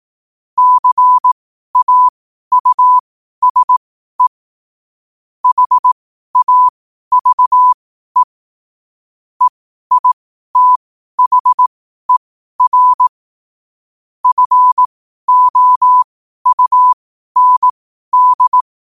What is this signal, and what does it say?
Morse code, heard as a keyed tone, about 18 wpm, spaced as for 12 wpm, 1000 Hz. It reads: CAUSE HAVE EITHER FOUND
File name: qotdfiles / 12wpm